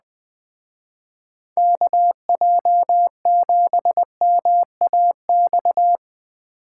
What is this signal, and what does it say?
Morse code, heard as a keyed tone, about 20 wpm, 700 Hz. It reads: KJ7MAX
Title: Speed Racing - USA 2x3 Call Signs 17wpm